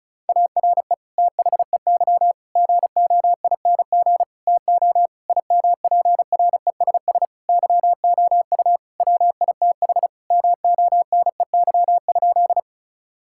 Morse code, heard as a keyed tone, about 34 wpm, 700 Hz. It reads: ARE THEY GOING TO IMPRESS YOU WITH MONEY?
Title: Sentences from Top 1600 Words 35wpm